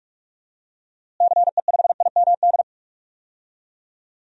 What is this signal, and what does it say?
Morse code, heard as a keyed tone, about 46 wpm, 700 Hz. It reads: CE5IMD
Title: DXCC-100 Call Signs - 6 Characters - Repeated 2x - 45wpm